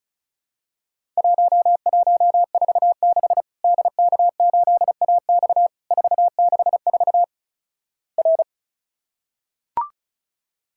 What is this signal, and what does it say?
Morse code, heard as a keyed tone, about 36 wpm, 700 Hz. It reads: 1146 DK8AX 464 R
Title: Worked All Europe DX Contest - QTC Exchange - 35wpm